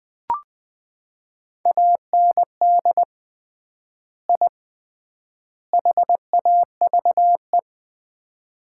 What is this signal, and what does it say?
Morse code, heard as a keyed tone, about 20 wpm, 700 Hz. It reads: AND I HAVE
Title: 3x Word-Spacing - Sets of 3 Words - 20wpm